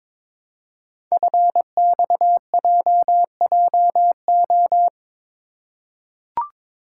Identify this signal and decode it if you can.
Morse code, heard as a keyed tone, about 22 wpm, 700 Hz. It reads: FXJJO
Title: Code Groups - 5 Letters - 22wpm